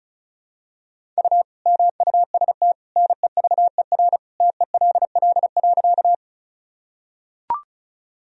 Morse code, heard as a keyed tone, about 36 wpm, 700 Hz. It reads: U MUST NEVER TELL.
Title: Sentences from Top 300 Words and Abbreviations 35wpm